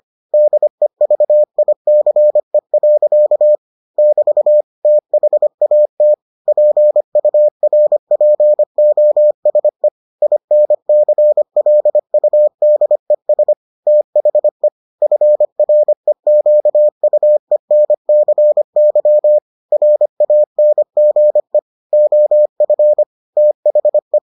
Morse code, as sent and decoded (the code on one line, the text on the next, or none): -.. ...- .. -.-. .-.-.- / -...- / - .... .- - / .--. ..- .-. .--. --- ... / .. -. -.-. .-.. ..- -.. ... / - .... / ..-. .-. --.- ..- . -. -.-. -.-- / .-. .- -. --. / --- ..-. / - .... .
DEVICE. <BT> THAT PURPOSE INCLUDES THE FREQUENCY RANGE OF THE